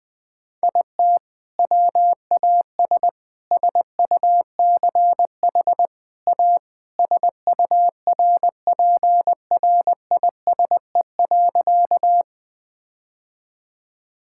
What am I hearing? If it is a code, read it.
Morse code, heard as a keyed tone, about 21 wpm, 700 Hz. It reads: IT WAS SUCH A SURPRISE.
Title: Sentences with 2-Word Phrases - Spoken Beginning - 20wpm